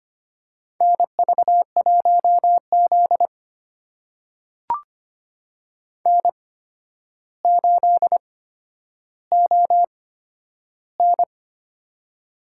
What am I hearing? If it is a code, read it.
Morse code, heard as a keyed tone, about 25 wpm, spaced as for 11 wpm, 700 Hz. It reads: NV1Z N8ON
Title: USA Call Signs - 4 Characters - 8x Character Spacing - 25wpm